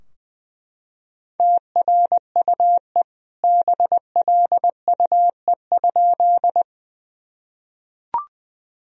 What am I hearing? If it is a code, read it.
Morse code, heard as a keyed tone, about 21 wpm, 700 Hz. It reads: TRUE BLUE?
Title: Sentences from Top 1000 Words - Review Part 3 - 20wpm